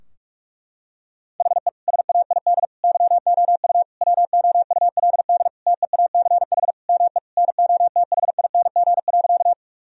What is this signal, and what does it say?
Morse code, heard as a keyed tone, about 46 wpm, 700 Hz. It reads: HE SAID YOU WOULD TEACH ME NOTHING.
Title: Sentences from Top 1000 Words - Review Part 2 - 45wpm